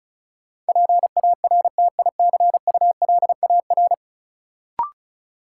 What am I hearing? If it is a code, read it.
Morse code, heard as a keyed tone, about 36 wpm, 700 Hz. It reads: PARTICULAR
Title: Sentences from Top 600 Words - New Word Drill - 35wpm